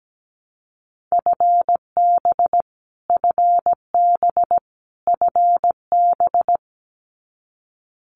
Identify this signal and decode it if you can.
Morse code, heard as a keyed tone, about 17 wpm, 700 Hz. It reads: FB FB FB
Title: Abbreviations Repeated 3x - 17wpm